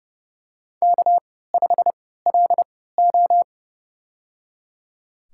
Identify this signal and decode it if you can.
Morse code, heard as a keyed tone, about 30 wpm, spaced as for 17 wpm, 700 Hz. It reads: K5LO
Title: USA Call Signs - 3x Character Spacing - 30wpm